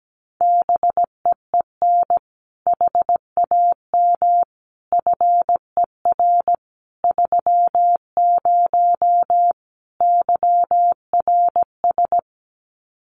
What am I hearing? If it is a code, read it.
Morse code, heard as a keyed tone, about 17 wpm, 700 Hz. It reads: BEEN HAM FER 30 YRS
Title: Common QSO Phrases Encore 17wpm